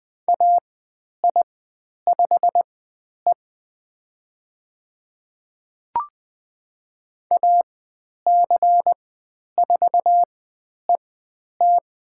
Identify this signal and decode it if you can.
Morse code, heard as a keyed tone, about 20 wpm, spaced as for 10 wpm, 700 Hz. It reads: AI5E AC4ET
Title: 095 Lesson 16g - Letter c - Call Signs 20wpm